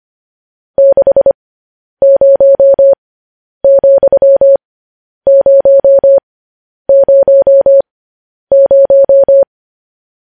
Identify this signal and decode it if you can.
Morse code, heard as a keyed tone, about 25 wpm, spaced as for 10 wpm, 550 Hz. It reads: 60,000
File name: podcast / News headlines-1x-25x10